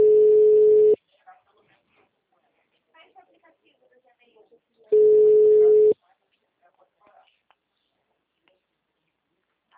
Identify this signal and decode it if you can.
Ringback tone (Europe, 425 Hz), with speech.
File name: out- -2002- - - .54003